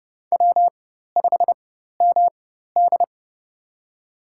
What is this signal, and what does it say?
Morse code, heard as a keyed tone, about 29 wpm, spaced as for 14 wpm, 700 Hz. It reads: W5MD
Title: USA Call Signs - 4x Character Spacing - 30wpm